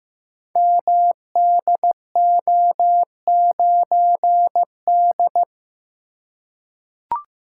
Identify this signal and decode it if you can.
Morse code, heard as a keyed tone, about 15 wpm, 700 Hz. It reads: MDO9D
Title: Code Groups - 5 Letter:Numbers - 15wpm